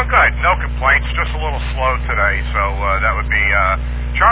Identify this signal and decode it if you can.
Emergency Alert System (SAME) tones, heard as harsh buzzing bursts, with speech. It is heard from 0.8 to 3.6 s.